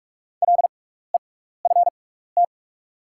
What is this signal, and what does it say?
Morse code, heard as a keyed tone, about 45 wpm, spaced as for 15 wpm, 700 Hz. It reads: LEFT
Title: 6x Character Spacing - Top 500 Words - 45wpm